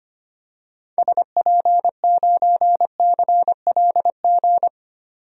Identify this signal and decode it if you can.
Morse code, heard as a keyed tone, about 25 wpm, 700 Hz. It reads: SP9CLG
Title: DXCC-100 Call Signs - 6 Characters - 25wpm